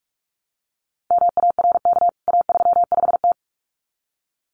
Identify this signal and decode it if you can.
Morse code, heard as a keyed tone, about 44 wpm, 700 Hz. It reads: MARK A35T
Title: CWT Exchange 45wpm